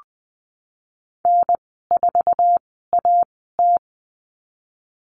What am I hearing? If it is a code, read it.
Morse code, heard as a keyed tone, about 20 wpm, spaced as for 15 wpm, 700 Hz. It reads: N4AT
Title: 048 Lesson 9g - Number 4 - Call Signs 20wpm